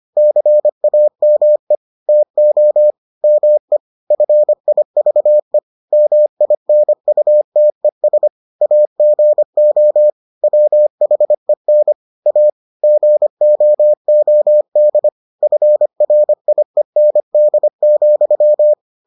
Morse code, heard as a keyed tone, about 25 wpm, 600 Hz. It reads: CAME TO ME FIVE MINUTES AGO WHEN A GOOD FRIEND,